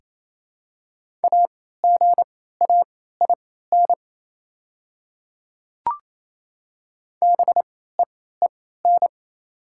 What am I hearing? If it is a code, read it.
Morse code, heard as a keyed tone, about 27 wpm, spaced as for 16 wpm, 700 Hz. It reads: AGAIN BEEN
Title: 3x Character Spacing - Top 200 Words - 28wpm